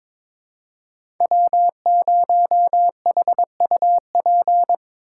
Morse code, heard as a keyed tone, about 22 wpm, 700 Hz. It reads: W0HUP
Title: USA Call Signs - 4x Character Spacing - 22wpm